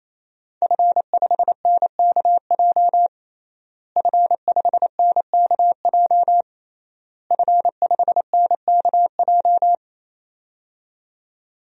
Morse code, heard as a keyed tone, about 28 wpm, 700 Hz. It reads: F5NKJ F5NKJ F5NKJ
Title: DXCC-100 Call Signs - 5 Characters - Repeated 3x - 28wpm